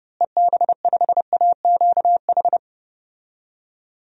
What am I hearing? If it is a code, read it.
Morse code, heard as a keyed tone, about 30 wpm, 700 Hz. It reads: EB5AQH
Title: DXCC-100 Call Signs - 6 Characters - Repeated 3x - 30wpm